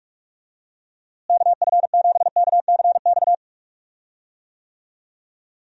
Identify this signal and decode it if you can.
Morse code, heard as a keyed tone, about 46 wpm, 700 Hz. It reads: KF7KCX
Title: USA Call Signs - 6 Characters - Repeated 3x - 45wpm